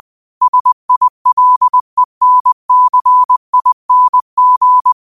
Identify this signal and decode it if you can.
Morse code, heard as a keyed tone, about 20 wpm, 1000 Hz. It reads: SILENCING
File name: qotdfiles / 20wpm